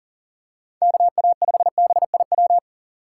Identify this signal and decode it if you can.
Morse code, heard as a keyed tone, about 40 wpm, 700 Hz. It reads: KA5BIW
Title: USA Call Signs - 6 Characters - 6x Character Spacing - 40wpm